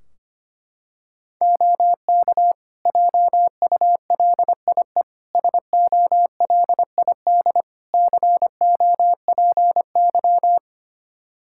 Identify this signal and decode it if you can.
Morse code, heard as a keyed tone, about 25 wpm, 700 Hz. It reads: OK JULIE SOLID COPY
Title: Practice QSO - Exchange 3 - More Info - 25wpm